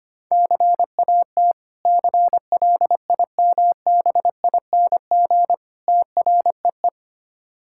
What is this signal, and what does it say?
Morse code, heard as a keyed tone, about 25 wpm, 690 Hz. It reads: CAT CLIMBING TREE
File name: practice / 25wpm / Difficult Words with Hints - 25wpm